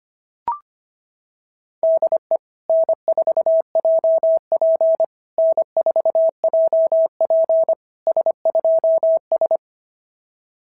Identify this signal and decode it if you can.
Morse code, heard as a keyed tone, about 25 wpm, 660 Hz. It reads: DE N4JP N4JP S2S
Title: SOTA QSO Practice 25wpm